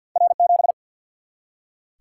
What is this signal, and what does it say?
Morse code, heard as a keyed tone, about 47 wpm, 700 Hz. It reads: R 7
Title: Single-Letter Single-Number 50wpm